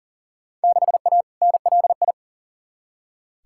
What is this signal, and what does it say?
Morse code, heard as a keyed tone, about 38 wpm, 700 Hz. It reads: 6A NLI
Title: Field Day Exchanges Encore - 40wpm